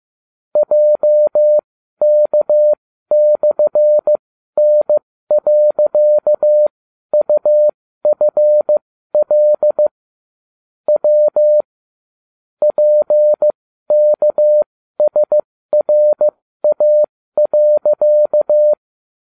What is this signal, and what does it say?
Morse code, heard as a keyed tone, about 15 wpm, spaced as for 12 wpm, 600 Hz. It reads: JK/N.UFL W PKSRA.